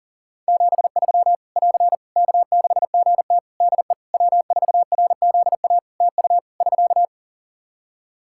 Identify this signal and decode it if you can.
Morse code, heard as a keyed tone, about 40 wpm, 700 Hz. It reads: 73 <AR> K6GT DE W4RZA TU <SK>
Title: Practice QSO - Exchange 4 - Ending - 40wpm